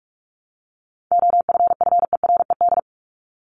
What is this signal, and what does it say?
Morse code, heard as a keyed tone, about 44 wpm, 700 Hz. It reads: OFFERED
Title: ED Suffix - Top 100 Words with Base Words 45wpm